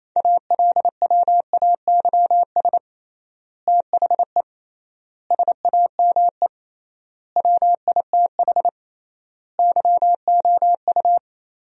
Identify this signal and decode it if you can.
Morse code, heard as a keyed tone, about 27 wpm, 700 Hz. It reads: ALWAYS THE SAME WITH YOU
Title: 3x Word-Spacing - Sentences from Top 300 Words - 28wpm